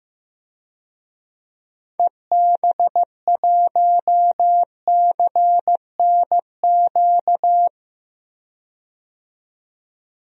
Morse code, heard as a keyed tone, about 15 wpm, 700 Hz. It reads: EB1CNQ
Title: DXCC-100 Call Signs - 6 Characters - 15wpm